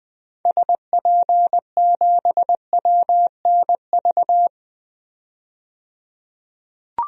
Morse code, heard as a keyed tone, about 20 wpm, 700 Hz. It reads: SP7WNV
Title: DXCC-100 Call Signs - 6 Characters - 20wpm